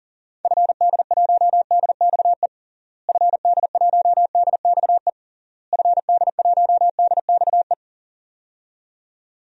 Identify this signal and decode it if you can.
Morse code, heard as a keyed tone, about 40 wpm, 700 Hz. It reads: FD1DXE FD1DXE FD1DXE
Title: DXCC-100 Call Signs - 6 Characters - Repeated 3x - 40wpm